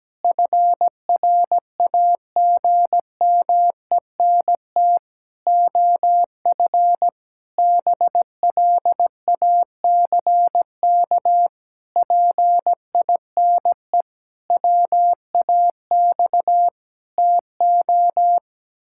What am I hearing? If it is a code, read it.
Morse code, heard as a keyed tone, about 17 wpm, 700 Hz. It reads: FRAGMENT OF BLACK PINE WAX TO
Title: Pangram Challenge v2 - 17wpm